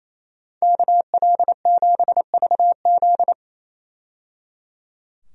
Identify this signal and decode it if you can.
Morse code, heard as a keyed tone, about 27 wpm, 700 Hz. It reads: KL7VZ
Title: DXCC-100 Call Signs - 5 Characters - 28wpm